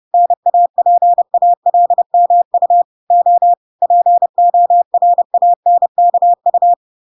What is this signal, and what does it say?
Morse code, heard as a keyed tone, about 30 wpm, 700 Hz. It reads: NAPALMU O PORANKU